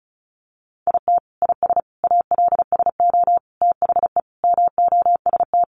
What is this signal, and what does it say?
Morse code, heard as a keyed tone, about 34 wpm, 700 Hz. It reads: IT IS ALSO THE MOST
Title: Sentences from Top 1400 Words 35wpm